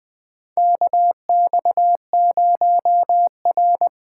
Morse code, heard as a keyed tone, about 21 wpm, 700 Hz. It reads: KX0R
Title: 2023 SOTA Chaser Call Signs - 20wpm